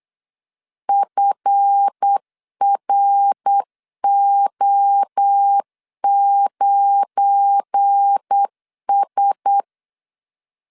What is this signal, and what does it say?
Morse code, heard as a keyed tone, about 8 wpm, 800 Hz. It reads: FRO9S
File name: CW / DET / 1 9 1500C S C 8wpm 5653